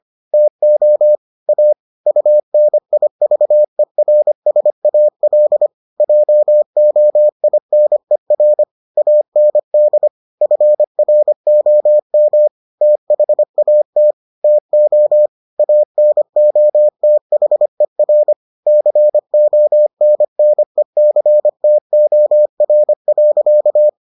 Morse code, heard as a keyed tone, about 25 wpm, 600 Hz. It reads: TO A UNIVERSAL JOINER AND FROM THAT TO ANOTHER CONNECTOR.